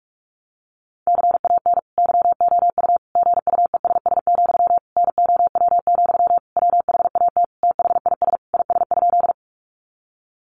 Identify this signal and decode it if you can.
Morse code, heard as a keyed tone, about 44 wpm, 700 Hz. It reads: CAN YOU GUESS, NOW, WHAT THIS IS?
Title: Sentences from Top 1500 Words 45wpm